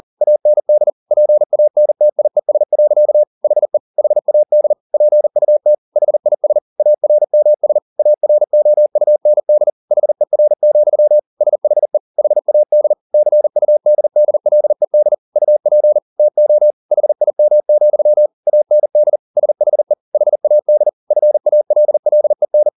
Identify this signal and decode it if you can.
Morse code, heard as a keyed tone, about 40 wpm, 600 Hz. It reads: AND PANTIES. HE HAD PUT HIS ARMS AROUND HER, SHE HAD CUDDLED UP TO HIM, AND SHE HAD FALLEN